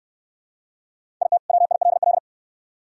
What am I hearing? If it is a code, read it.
Morse code, heard as a keyed tone, about 68 wpm, 700 Hz. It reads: IT WILL